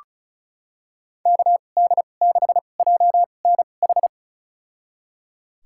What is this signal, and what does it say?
Morse code, heard as a keyed tone, about 36 wpm, spaced as for 25 wpm, 700 Hz. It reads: KD6JNH
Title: USA Call Signs - 2x Character Spacing - 35wpm